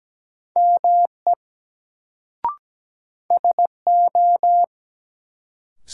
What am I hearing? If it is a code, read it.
Morse code, heard as a keyed tone, about 17 wpm, 700 Hz. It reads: ME SO
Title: Sets of 2 Words - New Word Drill - 17wpm